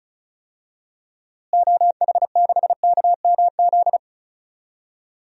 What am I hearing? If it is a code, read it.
Morse code, heard as a keyed tone, about 34 wpm, 700 Hz. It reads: OH6KMZ
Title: DXCC-100 Call Signs - 6 Characters - Repeated 3x - 35wpm